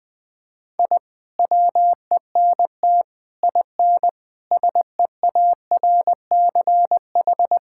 Morse code, heard as a keyed tone, about 20 wpm, 700 Hz. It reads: I WENT IN SEARCH
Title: Sentences from Top 1000 Words - Review Part 5 - 20wpm